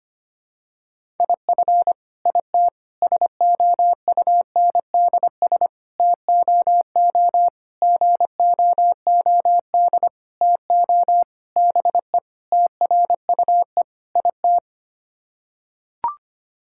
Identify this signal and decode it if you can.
Morse code, heard as a keyed tone, about 25 wpm, 700 Hz. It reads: IF IT SOUNDS TOO GOOD TO BE TRUE IT
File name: Sets of 9 Words - 25wpm